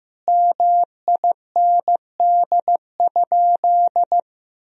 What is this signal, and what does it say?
Morse code, heard as a keyed tone, about 15 wpm, 700 Hz. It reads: MIND?
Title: Sentences from Top 1000 Words - Review Part 1 - 15wpm